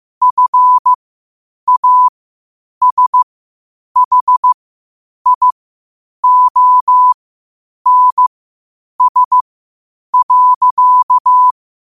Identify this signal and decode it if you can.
Morse code, heard as a keyed tone, about 15 wpm, spaced as for 8 wpm, 1000 Hz. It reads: FASHIONS.
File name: qotdfiles / 8wpm